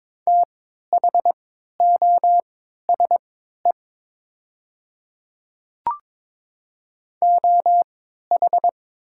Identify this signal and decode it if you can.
Morse code, heard as a keyed tone, about 23 wpm, spaced as for 13 wpm, 700 Hz. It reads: THOSE OH
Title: 3x Character Spacing - Top 500 Words - 22wpm